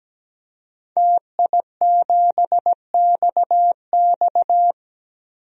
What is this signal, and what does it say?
Morse code, heard as a keyed tone, about 17 wpm, 700 Hz. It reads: TI7XX
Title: DXCC-100 Call Signs - 5 Characters - 17wpm